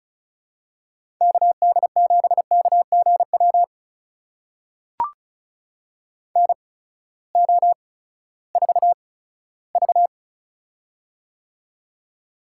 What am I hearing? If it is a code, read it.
Morse code, heard as a keyed tone, about 34 wpm, spaced as for 17 wpm, 700 Hz. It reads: KD7KGW NO4V
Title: USA Call Signs - 8x Character Spacing - 35wpm